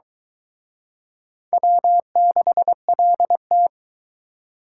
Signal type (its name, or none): Morse code